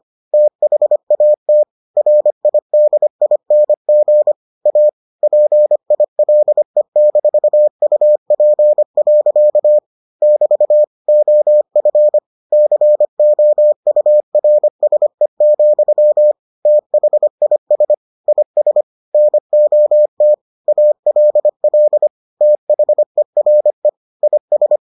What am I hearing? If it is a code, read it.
Morse code, heard as a keyed tone, about 25 wpm, 600 Hz. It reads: THAT RIDING A PILE-UP. <BT> OF COURSE, THIS IS NOT ALL THERE IS